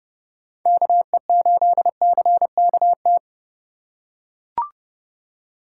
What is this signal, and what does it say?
Morse code, heard as a keyed tone, about 30 wpm, 700 Hz. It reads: KE8CKT